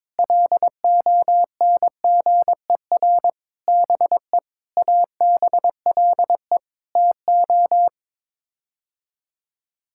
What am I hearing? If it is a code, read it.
Morse code, heard as a keyed tone, about 22 wpm, 700 Hz. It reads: LONGER BE ABLE TO
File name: practice / 22wpm / Sets of 6 Words - 22wpm